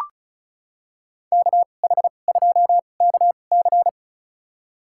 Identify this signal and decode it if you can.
Morse code, heard as a keyed tone, about 34 wpm, spaced as for 25 wpm, 700 Hz. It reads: KH2KC
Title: USA Call Signs - 2x Character Spacing - 35wpm